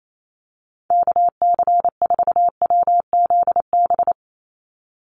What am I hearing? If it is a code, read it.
Morse code, heard as a keyed tone, about 28 wpm, 700 Hz. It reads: KC4WZB
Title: NA SOTA Activator Callsigns 28wpm